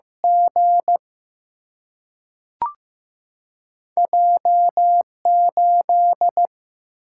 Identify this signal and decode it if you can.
Morse code, heard as a keyed tone, about 15 wpm, 700 Hz. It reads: G J 8
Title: Single-Letter Single-Number 15wpm